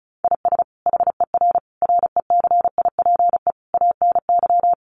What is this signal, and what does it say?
Morse code, heard as a keyed tone, about 34 wpm, 700 Hz. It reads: IS HER RECIPE ANY
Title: Sentences from Top 1200 Words 35wpm